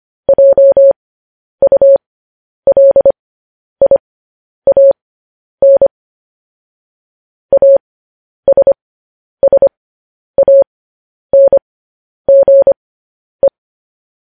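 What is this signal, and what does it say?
Morse code, heard as a keyed tone, about 25 wpm, spaced as for 10 wpm, 550 Hz. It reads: JULIAN ASSANGE
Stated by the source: Morse code practice sending news headlines at 10 wpm.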